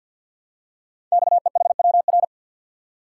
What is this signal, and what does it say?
Morse code, heard as a keyed tone, about 51 wpm, 700 Hz. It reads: XEHWR
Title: Code Groups - 5 Letters - 50wpm